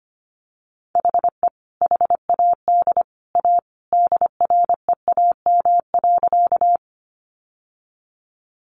Morse code, heard as a keyed tone, about 25 wpm, 700 Hz. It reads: HE HAD A DREAM.
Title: Sentences from Top 1500 Words 25wpm